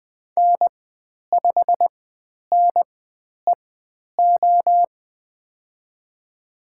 Morse code, heard as a keyed tone, about 20 wpm, spaced as for 10 wpm, 700 Hz. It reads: N5NEO
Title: 095 Lesson 16g - Letter c - Call Signs 20wpm